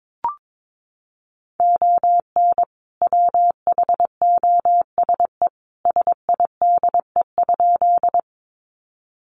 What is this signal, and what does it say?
Morse code, heard as a keyed tone, about 22 wpm, 700 Hz. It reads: ON WHOSE SIDE?
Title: Sentences from Top 1200 Words 22wpm